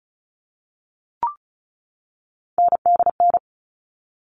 Morse code, heard as a keyed tone, about 34 wpm, 700 Hz. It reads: NDN
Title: All Cut Numbers - 3-Digit 35wpm